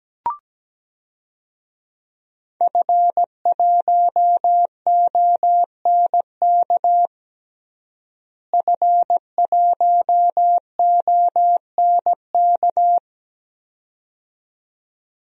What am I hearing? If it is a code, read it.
Morse code, heard as a keyed tone, about 17 wpm, 700 Hz. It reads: F1ONK F1ONK
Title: DXCC-100 Call Signs - 5 Characters - Repeated 2x - 17wpm